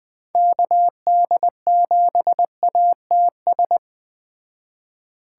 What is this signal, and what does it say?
Morse code, heard as a keyed tone, about 20 wpm, 700 Hz. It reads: KD7ATS
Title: Speed Racing - USA 2x3 Call Signs 17wpm